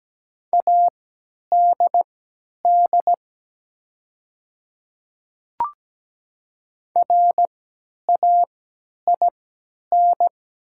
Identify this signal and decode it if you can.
Morse code, heard as a keyed tone, about 17 wpm, spaced as for 10 wpm, 700 Hz. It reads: ADD RAIN
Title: 3x Character Spacing - Top 500 Words - 17wpm